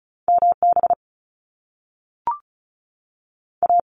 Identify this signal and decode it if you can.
Morse code, heard as a keyed tone, about 36 wpm, spaced as for 6 wpm, 700 Hz. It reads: MBA
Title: Canadian Provinces and Territories 35wpm